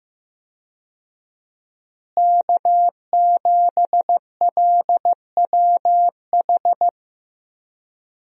Morse code, heard as a keyed tone, about 15 wpm, 700 Hz. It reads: K7LWH
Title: USA Call Signs - 5 Characters - Repeated 3x - 15wpm